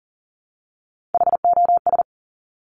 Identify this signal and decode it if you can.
Morse code, heard as a keyed tone, about 38 wpm, 700 Hz. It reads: HOS
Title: Top Three Letter Combinations in Words 40wpm